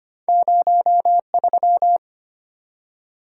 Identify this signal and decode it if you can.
Morse code, heard as a keyed tone, about 25 wpm, 700 Hz. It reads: 03
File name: Two Letter-Number Combinations - 25wpm